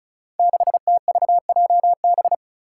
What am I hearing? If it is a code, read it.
Morse code, heard as a keyed tone, about 36 wpm, 700 Hz. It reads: 6TVJB
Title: Five Challenging Letter-Number Combinations 35wpm